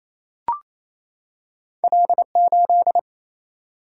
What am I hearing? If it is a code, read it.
Morse code, heard as a keyed tone, about 28 wpm, 700 Hz. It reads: L 8
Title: Single-Letter Single-Number 28wpm